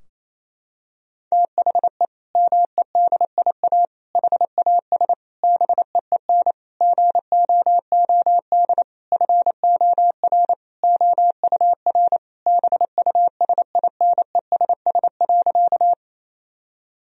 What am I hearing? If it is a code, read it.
Morse code, heard as a keyed tone, about 28 wpm, 700 Hz. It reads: THE MEDIA HAS BEEN GOOD FOR OUR BUSINESS.
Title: Sentences from Top 1000 Words - Review Part 4 - 28wpm